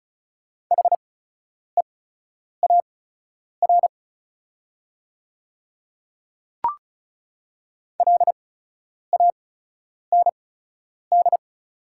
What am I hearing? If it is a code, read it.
Morse code, heard as a keyed tone, about 36 wpm, spaced as for 10 wpm, 700 Hz. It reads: HEAR LAND
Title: 8x Character Spacing - Top 300 Words - 35wpm